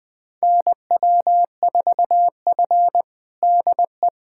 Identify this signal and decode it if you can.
Morse code, heard as a keyed tone, about 20 wpm, 700 Hz. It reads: NW4F DE
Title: USA Callsign DE Callsign - 4 Characters - 20wpm